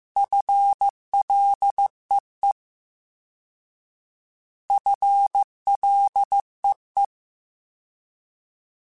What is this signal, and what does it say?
Morse code, heard as a keyed tone, about 15 wpm, 800 Hz. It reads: FLEE FLEE
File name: FL / FL mp3s / FL Prefixes 3x2 Twice @13